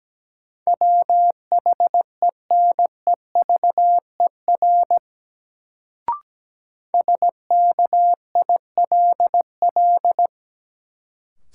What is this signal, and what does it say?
Morse code, heard as a keyed tone, about 17 wpm, 700 Hz. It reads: WHENEVER SKILL
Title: Sentences from Top 1200 Words - New Word Drill Alt - 17wpm